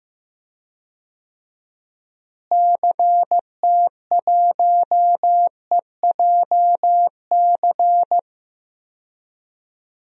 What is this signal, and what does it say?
Morse code, heard as a keyed tone, about 15 wpm, 700 Hz. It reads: CT1EJC
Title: DXCC-100 Call Signs - 6 Characters - Repeated 2x - 15wpm